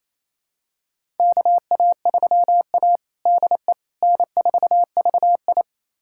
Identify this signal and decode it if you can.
Morse code, heard as a keyed tone, about 28 wpm, 700 Hz. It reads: KA3A DE N4VI
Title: USA Callsign DE Callsign - 4 Characters - 28wpm